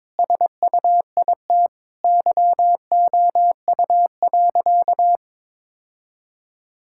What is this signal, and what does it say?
Morse code, heard as a keyed tone, about 22 wpm, 700 Hz. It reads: SUIT YOU.
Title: Sentences from Top 1600 Words 22wpm